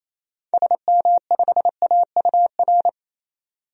Morse code, heard as a keyed tone, about 28 wpm, 700 Hz. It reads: SM5AUR
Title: DXCC-100 Call Signs - 6 Characters - 28wpm